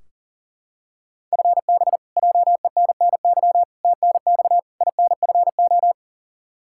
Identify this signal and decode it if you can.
Morse code, heard as a keyed tone, about 40 wpm, 700 Hz. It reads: FB JENNY TNX INFO
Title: Practice QSO - Exchange 3 - More Info - 40wpm